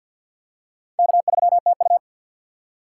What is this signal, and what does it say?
Morse code, heard as a keyed tone, about 52 wpm, 700 Hz. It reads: K3TU
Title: USA Call Signs - 4 Characters - 2x Character Spacing - 50wpm